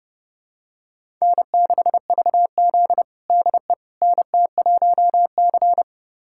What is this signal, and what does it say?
Morse code, heard as a keyed tone, about 32 wpm, 700 Hz. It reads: N6VZ DE NT1C
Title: USA Callsign DE Callsign - 4 Characters - 30wpm